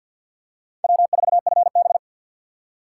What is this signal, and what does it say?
Morse code, heard as a keyed tone, about 51 wpm, 700 Hz. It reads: W4FB
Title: USA Call Signs - 4x Character Spacing - 50wpm